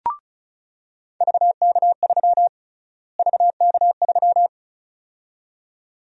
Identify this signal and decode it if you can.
Morse code, heard as a keyed tone, about 36 wpm, 700 Hz. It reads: VK3 VK3
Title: USA Call Sign Prefixes - Repeated 2x - 35wpm